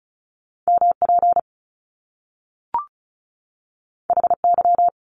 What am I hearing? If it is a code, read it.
Morse code, heard as a keyed tone, about 36 wpm, 700 Hz. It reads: MP HY
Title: Top Two Letter Combinations in Words 35wpm